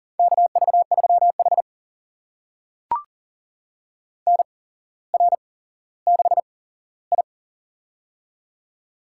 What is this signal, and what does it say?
Morse code, heard as a keyed tone, about 37 wpm, spaced as for 17 wpm, 700 Hz. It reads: KV3H NR6I
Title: USA Call Signs - 4 Characters - 8x Character Spacing - 40wpm